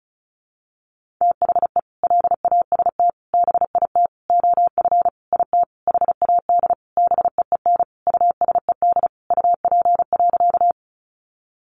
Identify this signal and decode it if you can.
Morse code, heard as a keyed tone, about 34 wpm, 700 Hz. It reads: THE LAST BIT OF IT HAD BEEN USED UP.
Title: Sentences from Top 900 Words 35wpm